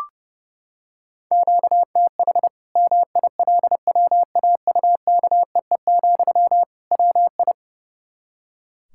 Morse code, heard as a keyed tone, about 32 wpm, 700 Hz. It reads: QTH MILWAUKEE, WI
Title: Practice QSO - Exchange 1 - Beginning - 30wpm